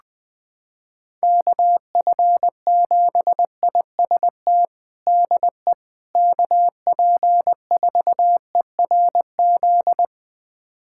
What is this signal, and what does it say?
Morse code, heard as a keyed tone, about 20 wpm, 700 Hz. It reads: KF7IST DE KP4ERZ
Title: USA Callsign DE Callsign - 6 Characters - 20wpm